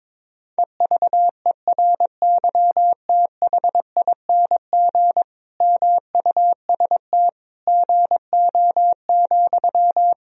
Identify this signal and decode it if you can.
Morse code, heard as a keyed tone, about 21 wpm, 700 Hz. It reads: EVERYTHING MUST GO,
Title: Sentences from Top 1000 Words - Review Part 4 - 22wpm